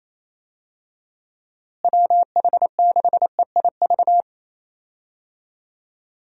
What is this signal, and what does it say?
Morse code, heard as a keyed tone, about 28 wpm, 700 Hz. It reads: WH6EIV
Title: USA Call Signs - 6 Characters - Repeated 3x - 28wpm